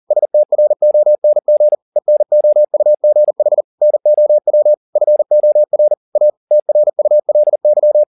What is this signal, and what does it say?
Morse code, heard as a keyed tone, about 40 wpm, 600 Hz. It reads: STRONG ENOUGH NOW FOR A TRULY